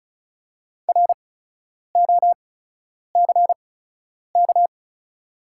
Morse code, heard as a keyed tone, about 36 wpm, spaced as for 10 wpm, 690 Hz. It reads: ROCK
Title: 8x Character Spacing - Top 500 Words - 35wpm